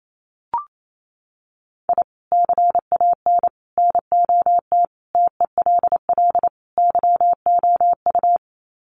Morse code, heard as a keyed tone, about 28 wpm, 700 Hz. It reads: I CAN NOT TELL YOU
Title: Sets of 5 Words 28wpm